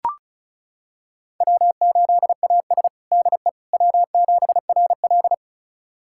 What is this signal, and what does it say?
Morse code, heard as a keyed tone, about 36 wpm, 700 Hz. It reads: W8AS DE W7RL
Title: USA Callsign DE Callsign - 4 Characters - 35wpm